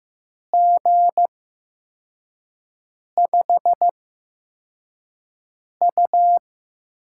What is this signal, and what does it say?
Morse code, heard as a keyed tone, about 15 wpm, spaced as for 4 wpm, 700 Hz. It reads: G5U
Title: USA Call Signs - 4 Characters - 8x Character Spacing - 15wpm